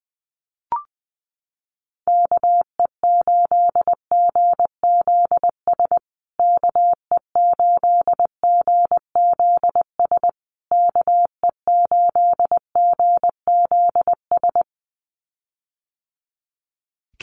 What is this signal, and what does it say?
Morse code, heard as a keyed tone, about 20 wpm, 700 Hz. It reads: KE8GZS KE8GZS KE8GZS
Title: USA Call Signs Repeated 3x 20wpm